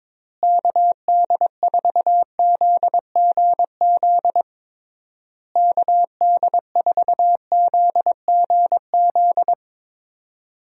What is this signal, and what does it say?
Morse code, heard as a keyed tone, about 22 wpm, 700 Hz. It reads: KD4ZGZ KD4ZGZ
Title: USA Call Signs - 6 Characters - Repeated 3x - 22wpm